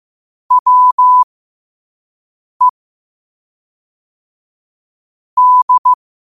Morse code, heard as a keyed tone, about 15 wpm, spaced as for 5 wpm, 1000 Hz. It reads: WE D
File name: qotdfiles / 5wpm